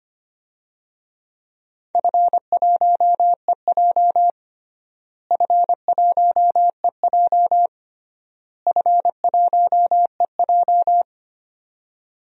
Morse code, heard as a keyed tone, about 25 wpm, 700 Hz. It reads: F1EJ F1EJ F1EJ
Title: DXCC-100 Call Signs - 4 Characters - Repeated 3x - 25wpm